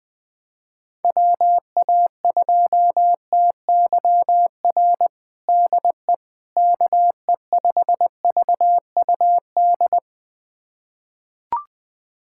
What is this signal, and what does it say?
Morse code, heard as a keyed tone, about 20 wpm, 700 Hz. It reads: WA2TYR DE KE5VUD
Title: USA Callsign DE Callsign - 6 Characters - 20wpm